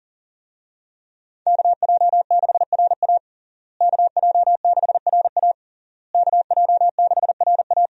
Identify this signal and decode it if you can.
Morse code, heard as a keyed tone, about 40 wpm, 690 Hz. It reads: KJ6RA KJ6RA KJ6RA
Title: USA Call Signs - 5 Characters - Repeated 3x - 40wpm